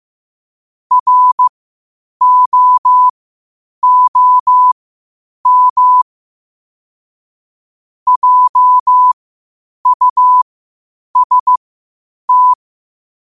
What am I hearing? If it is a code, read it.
Morse code, heard as a keyed tone, about 15 wpm, spaced as for 8 wpm, 1000 Hz. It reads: ROOM JUST
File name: qotdfiles / 8wpm